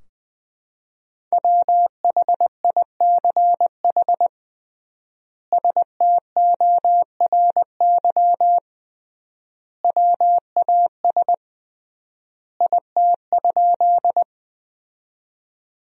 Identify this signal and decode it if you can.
Morse code, heard as a keyed tone, about 21 wpm, 700 Hz. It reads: WHICH STORY WAS IT?
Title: 3x Word-Spacing - Sentences from Top 300 Words - 20wpm